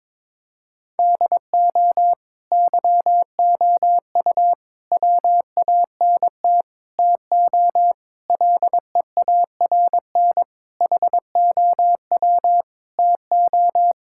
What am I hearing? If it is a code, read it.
Morse code, heard as a keyed tone, about 22 wpm, 700 Hz. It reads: DO YOU WANT TO LEARN HOW TO
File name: Sentences with 7-Word Phrases - 22wpm